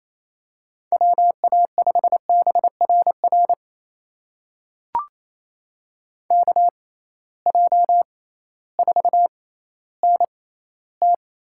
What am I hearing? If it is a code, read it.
Morse code, heard as a keyed tone, about 28 wpm, spaced as for 15 wpm, 700 Hz. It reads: WA5BRR KJ4NT
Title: USA Call Signs - 6x Character Spacing - 28wpm